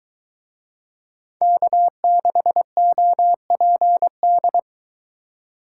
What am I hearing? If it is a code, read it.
Morse code, heard as a keyed tone, about 23 wpm, 700 Hz. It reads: K6OPD